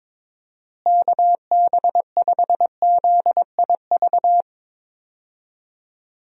Morse code, heard as a keyed tone, about 22 wpm, 700 Hz. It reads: KB5ZIV
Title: USA Call Signs - 6 Characters - Repeated 3x - 22wpm